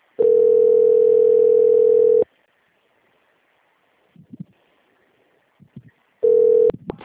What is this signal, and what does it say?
Ringback tone (North America), 440 + 480 Hz.